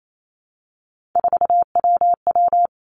Morse code, heard as a keyed tone, about 28 wpm, 700 Hz. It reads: 4WW
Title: Three Challenging Letter-Number Combinations 28wpm